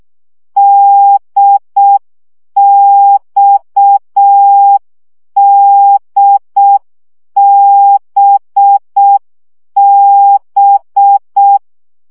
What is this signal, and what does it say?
Morse code, heard as a keyed tone, about 6 wpm, 800 Hz. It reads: DXDBB
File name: CW gyakorlas d b x 5 baud